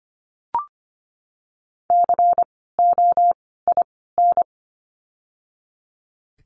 Morse code, heard as a keyed tone, about 25 wpm, 700 Hz. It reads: C O I N
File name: course / 25:10wpm / 094 Lesson 16f - Letter c - Words 25wpm